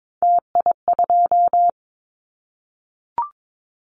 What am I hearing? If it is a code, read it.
Morse code, heard as a keyed tone, about 22 wpm, 700 Hz. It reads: TI2
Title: Callsign Prefixes for Top 100 DXCC - 22wpm